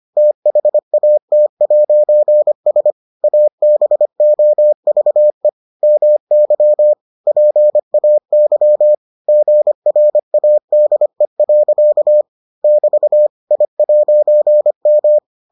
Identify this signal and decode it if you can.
Morse code, heard as a keyed tone, about 25 wpm, 600 Hz. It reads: THAT'S ABOVE MY PAY GRADE. <BT> I'M